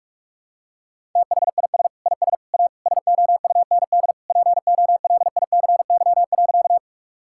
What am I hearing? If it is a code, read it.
Morse code, heard as a keyed tone, about 46 wpm, 700 Hz. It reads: THIS IS A SOUND POLICY.
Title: Sentences from Top 1000 Words - Review Part 5 - 45wpm